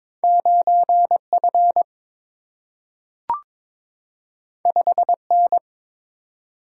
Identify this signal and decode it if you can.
Morse code, heard as a keyed tone, about 23 wpm, 700 Hz. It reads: 9 F 5 N
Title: Single-Letter Single-Number 22wpm